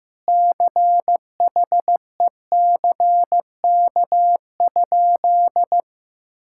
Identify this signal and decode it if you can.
Morse code, heard as a keyed tone, about 15 wpm, 700 Hz. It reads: CHECK?
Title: Sentences from Top 1000 Words - Review Part 6 - 15wpm